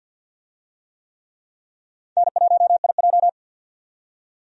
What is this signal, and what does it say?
Morse code, heard as a keyed tone, about 51 wpm, 700 Hz. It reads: N1IJ
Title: USA Call Signs - 4 Characters - 50wpm